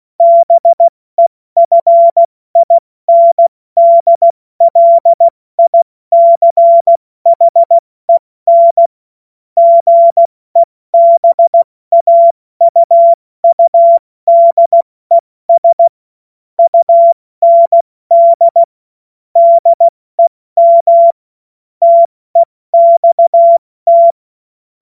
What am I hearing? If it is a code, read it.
Morse code, heard as a keyed tone, about 16 wpm, 680 Hz. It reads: BEFINDLICHEN GEBAUUDES UND DEM TEXT
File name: Morsekurs / Witze / Witz 23. Dezember 2025 70 F80